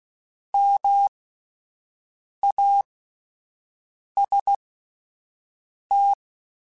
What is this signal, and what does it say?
Morse code, heard as a keyed tone, about 16 wpm, spaced as for 5 wpm, 800 Hz. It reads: MAST